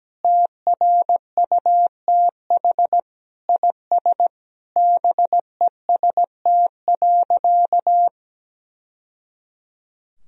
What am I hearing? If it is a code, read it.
Morse code, heard as a keyed tone, about 17 wpm, 700 Hz. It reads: TRUTH IS BEST.
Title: Sentences from Top 1000 Words - Review Part 2 - 17wpm